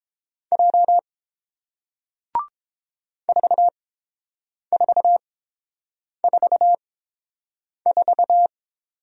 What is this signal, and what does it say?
Morse code, heard as a keyed tone, about 30 wpm, spaced as for 7 wpm, 700 Hz. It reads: J 4444